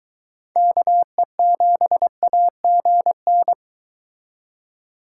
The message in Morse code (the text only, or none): KE7AGN